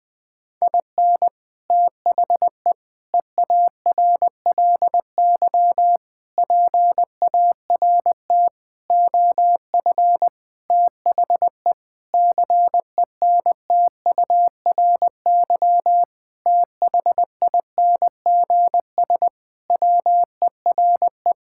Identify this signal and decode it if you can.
Morse code, heard as a keyed tone, about 21 wpm, 700 Hz. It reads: IN THE EARLY PART OF THE CENTURY THINGS WERE
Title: Sentences with 6-Word Phrases - Spoken Beginning - 20wpm